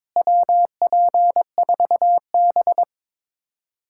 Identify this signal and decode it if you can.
Morse code, heard as a keyed tone, about 22 wpm, 700 Hz. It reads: WP4B
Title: USA Callsign DE Callsign - 4 Characters - 22wpm